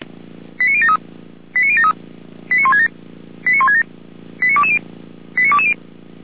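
Five-tone selective call (ZVEI), heard as quick runs of five tones, tones of about 70 ms. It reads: 89073 89073 89167 89167 89220 89220
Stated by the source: ZVEI five-tone selective call received on an SDR, from a fire brigade in Germany.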